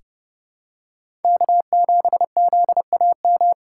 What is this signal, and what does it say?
Morse code, heard as a keyed tone, about 30 wpm, 700 Hz. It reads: K7ZAM
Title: USA Call Signs - 5 Characters - 4x Character Spacing - 30wpm